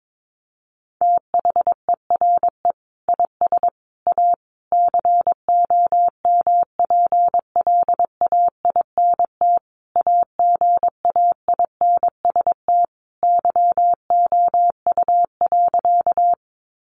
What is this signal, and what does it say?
Morse code, heard as a keyed tone, about 22 wpm, 700 Hz. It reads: THERE IS A COMPLAINT AGAINST YOU.
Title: Sentences from Top 1400 Words 22wpm